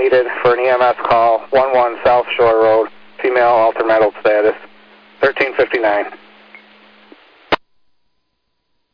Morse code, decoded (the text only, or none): none